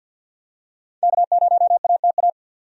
Morse code, heard as a keyed tone, about 50 wpm, 700 Hz. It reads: K0ATA
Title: USA Call Signs - 5 Characters - 50wpm